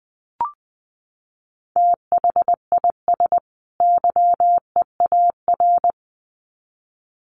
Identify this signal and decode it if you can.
Morse code, heard as a keyed tone, about 20 wpm, 700 Hz. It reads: THIS YEAR
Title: Sets of 2 Words 20wpm